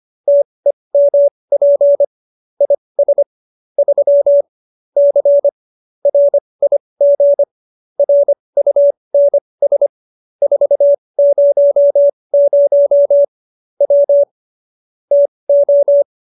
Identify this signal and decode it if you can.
Morse code, heard as a keyed tone, about 25 wpm, spaced as for 20 wpm, 570 Hz. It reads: TEMP IS 3 C RIG RUNS 400 W TO
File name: QSO310 20